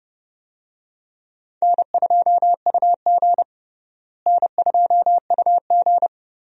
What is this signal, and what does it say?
Morse code, heard as a keyed tone, about 30 wpm, 700 Hz. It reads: N2UG N2UG
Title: USA Call Signs - 4 Characters - Repeated 2x - 30wpm